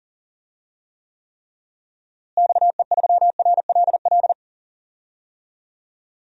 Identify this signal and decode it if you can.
Morse code, heard as a keyed tone, about 37 wpm, 700 Hz. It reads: XE3RLL
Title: DXCC-100 Call Signs - 6 Characters - Repeated 3x - 40wpm